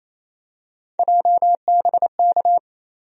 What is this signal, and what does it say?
Morse code, heard as a keyed tone, about 28 wpm, 700 Hz. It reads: JBK
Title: USA Call Sign Postfixes - 3 Letters 28wpm